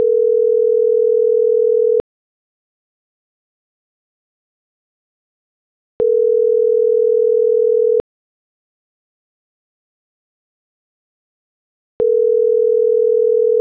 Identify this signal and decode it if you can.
Ringback tone (North America), 440 + 480 Hz.